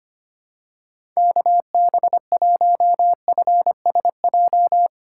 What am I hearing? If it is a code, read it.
Morse code, heard as a keyed tone, about 25 wpm, 700 Hz. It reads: KB1FSJ
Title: USA Call Signs - 3x Character Spacing - 25wpm